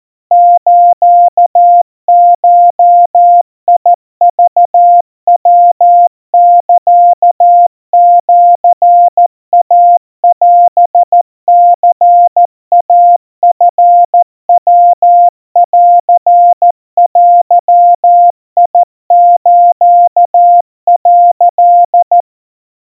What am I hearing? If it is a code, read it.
Morse code (Japanese Wabun code), heard as a keyed tone, about 14 wpm, 700 Hz. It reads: スゴクヤサシイオニイチヤンデス」
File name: kawa / Morse sample / kamyu 50 cpm 700 Hz Crispy